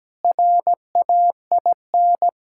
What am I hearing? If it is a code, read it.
Morse code, heard as a keyed tone, about 17 wpm, 700 Hz. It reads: RAIN
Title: Practice QSO - Exchange 2 - Info - 17wpm